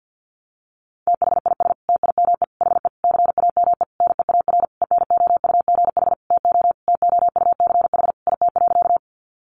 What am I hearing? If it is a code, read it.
Morse code, heard as a keyed tone, about 50 wpm, 700 Hz. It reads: THIS TIME HE CAME NEAR ENOUGH TO TOUCH IT.
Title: Sentences from Top 1200 Words 50wpm